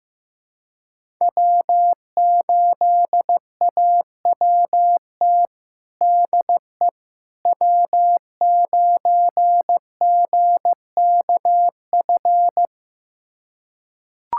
Morse code, heard as a keyed tone, about 15 wpm, 700 Hz. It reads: W8AWT DE W9GKF
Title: USA Callsign DE Callsign - 5 Characters - 15wpm